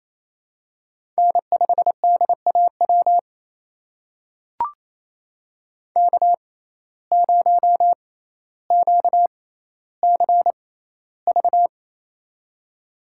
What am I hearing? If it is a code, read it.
Morse code, heard as a keyed tone, about 28 wpm, spaced as for 14 wpm, 700 Hz. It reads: N5DAW K0QCV
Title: USA Call Signs - 5 Characters - 6x Character Spacing - 28wpm